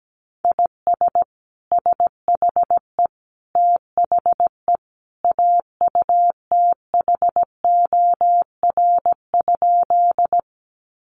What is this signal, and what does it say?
Morse code, heard as a keyed tone, about 17 wpm, 700 Hz. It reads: IS SHE THE AUTHOR?
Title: Sentences from Top 1200 Words 17wpm